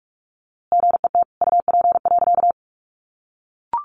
Morse code, heard as a keyed tone, about 44 wpm, 700 Hz. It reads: GET UP.
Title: A Sound Of Thunder 45wpm